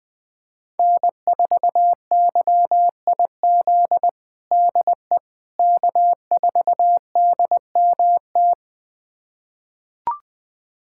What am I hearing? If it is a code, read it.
Morse code, heard as a keyed tone, about 20 wpm, 700 Hz. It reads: N4YIZ DE K4DMT
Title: USA Callsign DE Callsign - 5 Characters - 20wpm